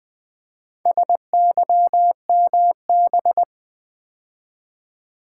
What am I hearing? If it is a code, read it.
Morse code, heard as a keyed tone, about 20 wpm, 690 Hz. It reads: SYMB
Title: Build A Word - 2 Letters At A Time - 6-Letter Words - 20wpm